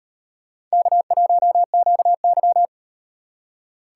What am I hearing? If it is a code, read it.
Morse code, heard as a keyed tone, about 35 wpm, 700 Hz. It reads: K1QY